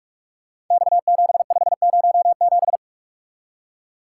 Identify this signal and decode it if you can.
Morse code, heard as a keyed tone, about 44 wpm, 700 Hz. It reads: X7507
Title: Code Groups - 5 Letter:Numbers - 45wpm